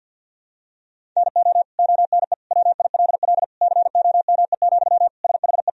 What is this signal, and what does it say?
Morse code, heard as a keyed tone, about 48 wpm, 700 Hz. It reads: NO ONE WILL COME, SHE
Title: Sentences from Top 100 Words and Abbreviations 50wpm